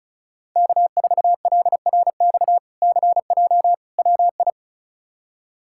Morse code, heard as a keyed tone, about 36 wpm, 700 Hz. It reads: K4LRX CJ WI
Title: K1USN SST QSO Practice - 35wpm